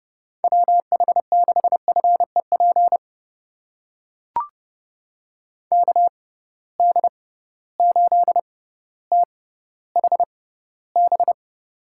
Morse code, heard as a keyed tone, about 31 wpm, spaced as for 16 wpm, 700 Hz. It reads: WH6FEP KD8THB
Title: USA Call Signs - 6 Characters - 6x Character Spacing - 30wpm